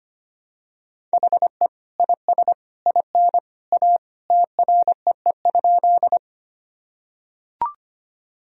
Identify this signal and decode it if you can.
Morse code, heard as a keyed tone, about 25 wpm, 700 Hz. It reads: HE IS IN A TREE?
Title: Sentences from Top 1000 Words - Review Part 1 - 25wpm